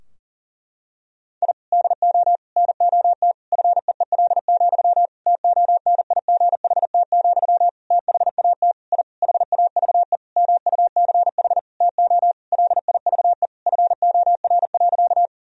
Morse code, heard as a keyed tone, about 37 wpm, 700 Hz. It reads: I DO NOT FEEL, TONIGHT, THAT I HAVE MUCH TO LIVE FOR.
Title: Sentences from Top 1000 Words - Review Part 3 - 40wpm